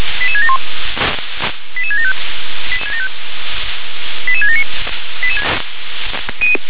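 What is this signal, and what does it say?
Five-tone selective call (ZVEI), heard as quick runs of five tones, tones of about 70 ms. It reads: 88671 88675 88679
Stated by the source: ZVEI five-tone selective call received with an SDR.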